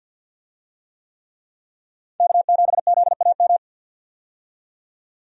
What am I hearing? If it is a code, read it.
Morse code, heard as a keyed tone, about 53 wpm, 700 Hz. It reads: K7ZAM
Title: USA Call Signs - 5 Characters - 50wpm